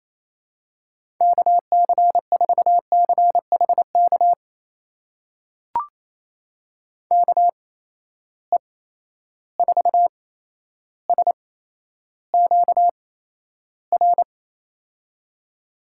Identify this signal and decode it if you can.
Morse code, heard as a keyed tone, about 28 wpm, spaced as for 12 wpm, 700 Hz. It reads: KC4CHK KE4SQR
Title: USA Call Signs - 6 Characters - 8x Character Spacing - 28wpm